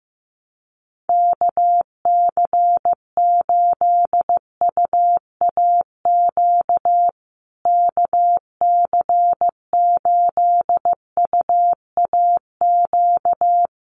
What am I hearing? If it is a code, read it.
Morse code, heard as a keyed tone, about 15 wpm, 700 Hz. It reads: KC8UAQ KC8UAQ
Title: USA Call Signs Repeated 3x 15wpm